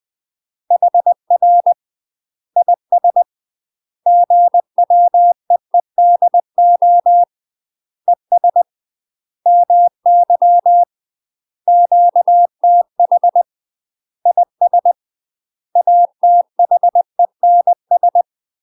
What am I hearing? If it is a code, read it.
Morse code, heard as a keyed tone, about 20 wpm, 700 Hz. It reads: HR IS GWEEDO ES MY QTH IS ATHENS
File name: UK MASTER QSOs TRAINING / 20WPM / UK MASTER QSOs 0007